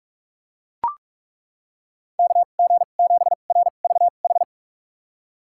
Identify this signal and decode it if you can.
Morse code, heard as a keyed tone, about 41 wpm, spaced as for 31 wpm, 700 Hz. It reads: KG7RVH
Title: USA Call Signs - 6 Characters - 2x Character Spacing - 45wpm